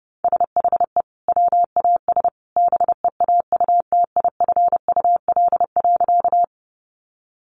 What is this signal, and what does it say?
Morse code, heard as a keyed tone, about 30 wpm, 700 Hz. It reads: SHE WAS BEAUTIFUL.
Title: Sentences from Top 900 Words 30wpm